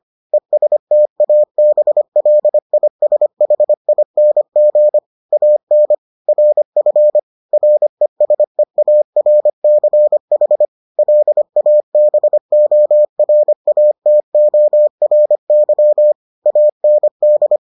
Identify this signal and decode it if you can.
Morse code, heard as a keyed tone, about 25 wpm, 600 Hz. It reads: ESTABLISHING AN RF RESEARCH LABORATORY AND